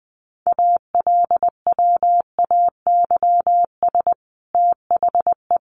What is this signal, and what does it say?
Morse code, heard as a keyed tone, about 20 wpm, 700 Hz. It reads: ALWAYS THE
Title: Sentences from Top 300 Words 20wpm